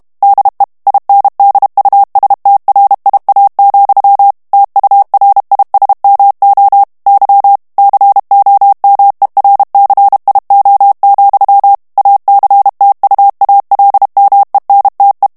Morse code, heard as a keyed tone, about 32 wpm, 800 Hz. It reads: DE INDUSTRIA, TURISMO Y COMERCIO, ACTUALMENTE
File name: ESPAGNOL TEXTE EN CLAIR 2 VITESSE 30 MOTS MINUTES